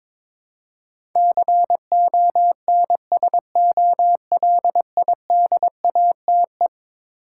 Morse code, heard as a keyed tone, about 22 wpm, 700 Hz. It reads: CONSOLIDATE
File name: practice / 22wpm / Eleven Letter Words - 22wpm